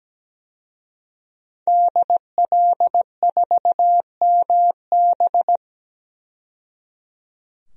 Morse code, heard as a keyed tone, about 17 wpm, 700 Hz. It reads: DL4MB
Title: DXCC-100 Call Signs - 5 Characters - 17wpm